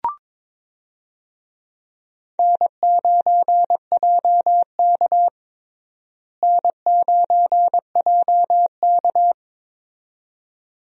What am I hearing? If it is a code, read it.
Morse code, heard as a keyed tone, about 22 wpm, 700 Hz. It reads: N9JK N9JK
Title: USA Call Signs - 4 Characters - Repeated 2x - 22wpm